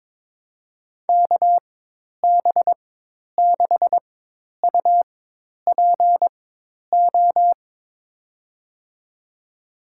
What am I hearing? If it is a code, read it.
Morse code, heard as a keyed tone, about 22 wpm, spaced as for 10 wpm, 700 Hz. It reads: KB6UPO
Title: USA Call Signs - 6 Characters - 4x Character Spacing - 22wpm